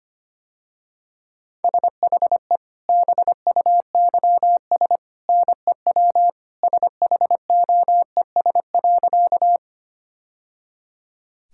Morse code, heard as a keyed tone, about 25 wpm, 690 Hz. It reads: SHE BUYS NEW SHOES.
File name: Easy Sentences with Hints - 25wpm